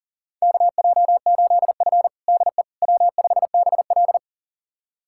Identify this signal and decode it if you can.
Morse code, heard as a keyed tone, about 40 wpm, 700 Hz. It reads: KJ8F DE W5BL
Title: USA Callsign DE Callsign - 4 Characters - 40wpm